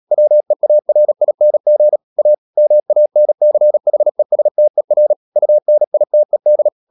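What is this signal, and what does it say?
Morse code, heard as a keyed tone, about 37 wpm, 600 Hz. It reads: WEARING A MANCHESTER UNITED
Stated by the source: Morse code practice at 35–39 wpm sending English text of Ken Follett's novel Never, from CW med Gnister.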